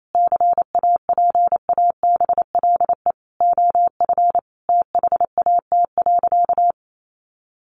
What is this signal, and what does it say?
Morse code, heard as a keyed tone, about 28 wpm, 700 Hz. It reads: CAPABLE OF THAT.
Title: Sentences from Top 1200 Words 28wpm